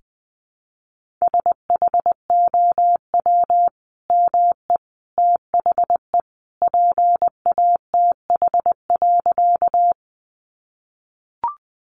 Morse code, heard as a keyed tone, about 20 wpm, 700 Hz. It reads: SHOW ME THE PATH.
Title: Sentences from Top 1400 Words 20wpm